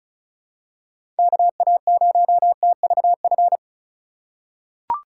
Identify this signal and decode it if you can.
Morse code, heard as a keyed tone, about 34 wpm, 700 Hz. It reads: KA0TVF
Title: USA Call Signs - 6 Characters - 3x Character Spacing - 35wpm